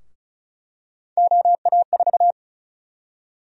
Morse code, heard as a keyed tone, about 36 wpm, 700 Hz. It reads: OA4
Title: USA Call Sign Prefixes - Repeated 2x - 35wpm